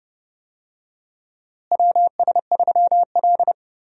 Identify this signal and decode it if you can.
Morse code, heard as a keyed tone, about 31 wpm, 700 Hz. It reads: WS3L
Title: USA Call Signs - 4 Characters - Repeated 2x - 30wpm